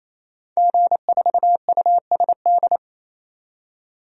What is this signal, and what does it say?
Morse code, heard as a keyed tone, about 27 wpm, 700 Hz. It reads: G4USD
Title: DXCC-100 Call Signs - 5 Characters - 28wpm